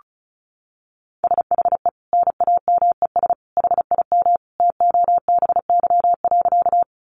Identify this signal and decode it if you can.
Morse code, heard as a keyed tone, about 34 wpm, 700 Hz. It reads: SHE NAMES HIM TOBY.
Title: Easy Phrases 35wpm